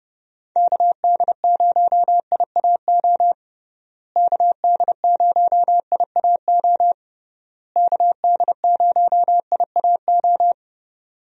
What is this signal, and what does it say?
Morse code, heard as a keyed tone, about 30 wpm, 700 Hz. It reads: KD0IAO KD0IAO KD0IAO
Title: USA Call Signs - 6 Characters - Repeated 3x - 30wpm